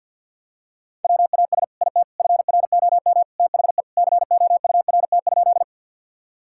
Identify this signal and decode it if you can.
Morse code, heard as a keyed tone, about 50 wpm, 700 Hz. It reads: WAS IT FROM THE COURT?
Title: Sentences from Top 1600 Words 50wpm